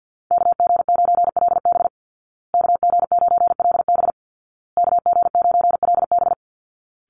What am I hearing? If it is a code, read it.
Morse code, heard as a keyed tone, about 50 wpm, 700 Hz. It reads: KG9LB KG9LB KG9LB
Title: Call Signs Repeated 2x - 50wpm